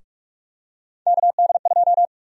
Morse code, heard as a keyed tone, about 46 wpm, 700 Hz. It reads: KD2
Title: USA Call Sign Prefixes - 45wpm